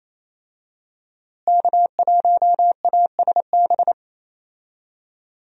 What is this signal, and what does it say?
Morse code, heard as a keyed tone, about 28 wpm, 700 Hz. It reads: K1ASB
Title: USA Call Signs - 5 Characters - Repeated 3x - 28wpm